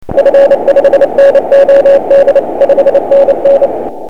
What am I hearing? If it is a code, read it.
Morse code, heard as a keyed tone, about 29 wpm, 560 Hz, with radio noise. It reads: F5NOD 5NN
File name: dxrecords / vu4anvu3tly 24cw